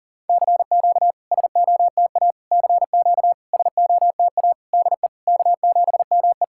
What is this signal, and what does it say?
Morse code, heard as a keyed tone, about 40 wpm, 690 Hz. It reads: CQ SOTA CQ SOTA DE K7ME
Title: SOTA QSO Practice 40wpm